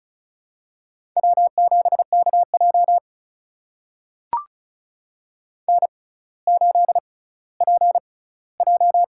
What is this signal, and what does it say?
Morse code, heard as a keyed tone, about 36 wpm, spaced as for 18 wpm, 700 Hz. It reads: W7KJ N8PJ
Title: USA Call Signs - 4 Characters - 6x Character Spacing - 35wpm